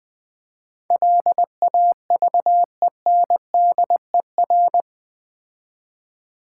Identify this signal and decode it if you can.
Morse code, heard as a keyed tone, about 20 wpm, 700 Hz. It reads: LAVENDER
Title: Difficult Words with Hints - 20wpm